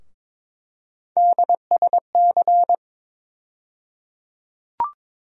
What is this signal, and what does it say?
Morse code, heard as a keyed tone, about 22 wpm, 700 Hz. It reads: DSC
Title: USA Call Sign Postfixes - 3 Letters 22wpm